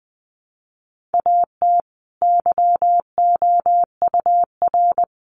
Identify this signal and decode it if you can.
Morse code, heard as a keyed tone, about 20 wpm, 700 Hz. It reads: AT YOUR
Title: Sentences from Top 1500 Words 20wpm